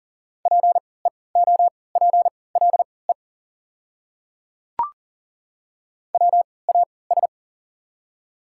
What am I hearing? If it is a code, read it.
Morse code, heard as a keyed tone, about 40 wpm, spaced as for 23 wpm, 700 Hz. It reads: PEOPLE WAS
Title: 3x Character Spacing - Top 200 Words - 40wpm